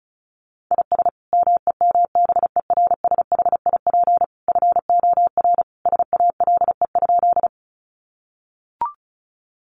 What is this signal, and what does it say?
Morse code, heard as a keyed tone, about 34 wpm, 700 Hz. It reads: IS MEMBERSHIP FOR SALE?
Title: Sentences from Top 1400 Words 35wpm